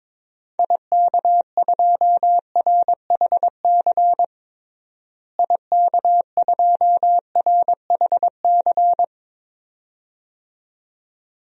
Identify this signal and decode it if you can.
Morse code, heard as a keyed tone, about 22 wpm, 700 Hz. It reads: IK2RHC IK2RHC
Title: DXCC-100 Call Signs - 6 Characters - Repeated 2x - 22wpm